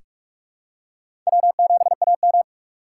Morse code, heard as a keyed tone, about 46 wpm, 700 Hz. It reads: W7AM
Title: USA Call Signs - 4 Characters - 6x Character Spacing - 45wpm